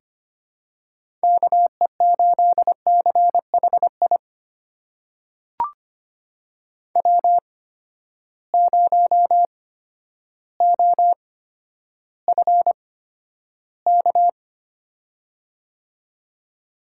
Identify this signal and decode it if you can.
Morse code, heard as a keyed tone, about 25 wpm, spaced as for 11 wpm, 700 Hz. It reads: KE8CHI W0OFK
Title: USA Call Signs - 8x Character Spacing - 25wpm